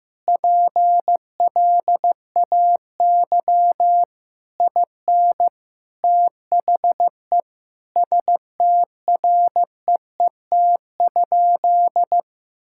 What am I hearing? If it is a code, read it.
Morse code, heard as a keyed tone, about 15 wpm, 700 Hz. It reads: PLAY IN THE STREET?
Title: Sentences from Top 1600 Words 15wpm